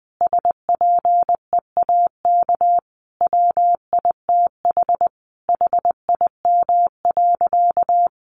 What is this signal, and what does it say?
Morse code, heard as a keyed tone, about 20 wpm, 700 Hz. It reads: SPEAK WITH HIM.
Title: Sentences from Top 1100 Words 20wpm